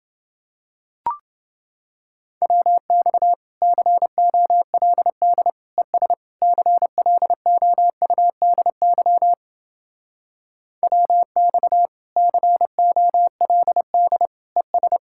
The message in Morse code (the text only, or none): WX COLD ES CLOUDY WX COLD ES